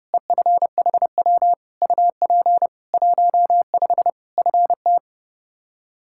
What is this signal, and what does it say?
Morse code, heard as a keyed tone, about 32 wpm, 700 Hz. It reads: EFHW UP 15 FT
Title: Practice QSO - Exchange 2 - Info - 30wpm